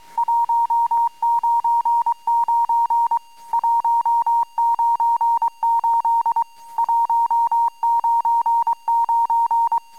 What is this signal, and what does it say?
Morse code, heard as a keyed tone, about 23 wpm, 910 Hz. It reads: 199 198 199
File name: amsat / cw